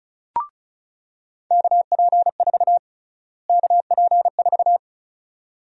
Morse code, heard as a keyed tone, about 36 wpm, 700 Hz. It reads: KP4 KP4
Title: USA Call Sign Prefixes - Repeated 2x - 35wpm